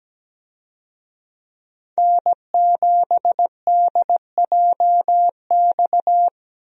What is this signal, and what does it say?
Morse code, heard as a keyed tone, about 17 wpm, 700 Hz. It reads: N7DJX
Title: USA Call Signs - 5 Characters - 17wpm